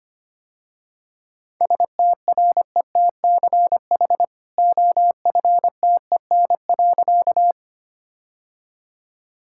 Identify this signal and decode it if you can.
Morse code, heard as a keyed tone, about 25 wpm, 700 Hz. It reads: STRETCH OFTEN.
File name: Easy Sentences with Hints - 25wpm